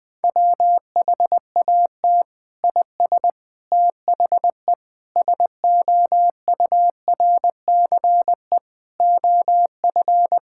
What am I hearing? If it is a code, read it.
Morse code, heard as a keyed tone, about 20 wpm, 700 Hz. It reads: WHAT IS THE SOURCE OF
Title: Sentences from Top 1000 Words - Review Part 5 - 20wpm